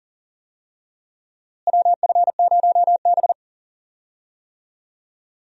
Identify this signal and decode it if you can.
Morse code, heard as a keyed tone, about 37 wpm, 700 Hz. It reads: WF0B
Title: USA Call Signs - 4 Characters - Repeated 3x - 40wpm